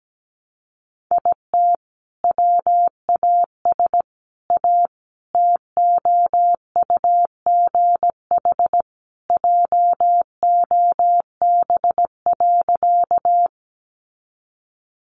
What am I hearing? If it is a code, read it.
Morse code, heard as a keyed tone, about 17 wpm, 700 Hz. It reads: IT WAS A TOUGH JOB.
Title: Sentences from Top 1200 Words 17wpm